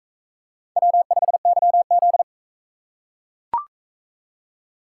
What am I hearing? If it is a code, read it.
Morse code, heard as a keyed tone, about 38 wpm, 700 Hz. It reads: W5YZ